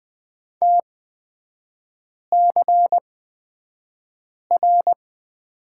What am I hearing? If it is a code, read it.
Morse code, heard as a keyed tone, about 20 wpm, spaced as for 5 wpm, 700 Hz. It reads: TCR
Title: 224 Lesson 36e - BT - Three Characters 20wpm